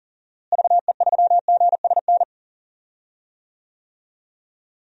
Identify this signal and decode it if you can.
Morse code, heard as a keyed tone, about 40 wpm, 700 Hz. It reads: VE3GSN
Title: DXCC-100 Call Signs - 6 Characters - Repeated 2x - 40wpm